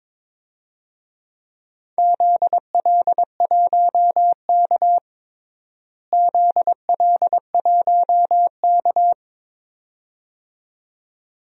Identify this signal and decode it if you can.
Morse code, heard as a keyed tone, about 22 wpm, 700 Hz. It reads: ZL1K ZL1K
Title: DXCC-100 Call Signs - 4 Characters - Repeated 2x - 22wpm